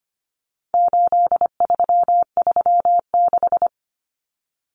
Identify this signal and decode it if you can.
Morse code, heard as a keyed tone, about 25 wpm, 700 Hz. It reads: 8336
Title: Common Cut Numbers - 4-Digit 25wpm